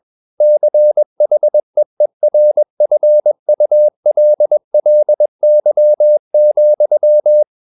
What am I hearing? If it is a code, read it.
Morse code, heard as a keyed tone, about 21 wpm, 600 Hz. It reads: CHEERFULLY,
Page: CW med Gnister